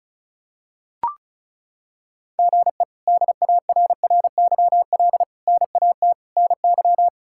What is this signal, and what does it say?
Morse code, heard as a keyed tone, about 34 wpm, 700 Hz. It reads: GE DARRYL NAT NY
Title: K1USN SST QSO Practice - 35wpm